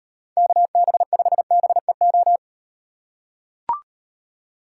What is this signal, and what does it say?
Morse code, heard as a keyed tone, about 38 wpm, 700 Hz. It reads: KB5BEO